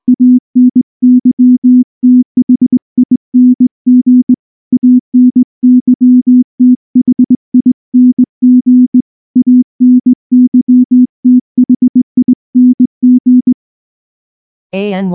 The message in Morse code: ANYTHING ANYTHING ANYTHING